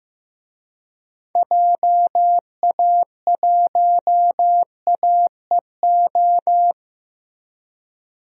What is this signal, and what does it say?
Morse code, heard as a keyed tone, about 15 wpm, 700 Hz. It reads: JA1AEO
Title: DXCC-100 Call Signs - 6 Characters - Repeated 2x - 15wpm